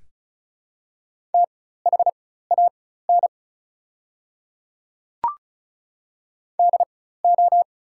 Morse code, heard as a keyed tone, about 36 wpm, spaced as for 16 wpm, 700 Hz. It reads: THAN DO
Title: 4x Character Spacing - Top 100 Words - 35wpm